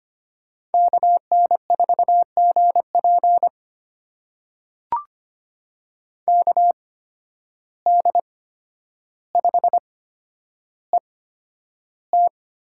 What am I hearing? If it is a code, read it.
Morse code, heard as a keyed tone, about 25 wpm, spaced as for 11 wpm, 700 Hz. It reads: KN4GP KD5ET
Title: USA Call Signs - 8x Character Spacing - 25wpm